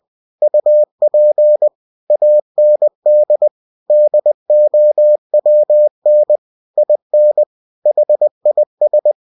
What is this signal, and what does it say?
Morse code, heard as a keyed tone, about 20 wpm, 600 Hz. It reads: UP AND DOWN IN HIS
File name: cw-books / winnie-the-pooh / winnie-20wpm-page-0106